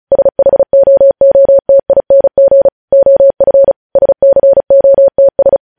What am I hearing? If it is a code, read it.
Morse code, heard as a keyed tone, about 36 wpm, 550 Hz. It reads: SHOOTING OF SCOTS
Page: news headlines in morse code at 35 wpm